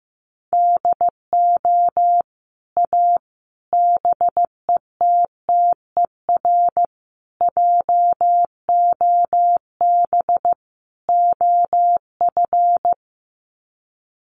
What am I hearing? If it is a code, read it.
Morse code, heard as a keyed tone, about 15 wpm, 700 Hz. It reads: DO A BETTER JOB OF
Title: Sets of 5 Words Encore2 15wpm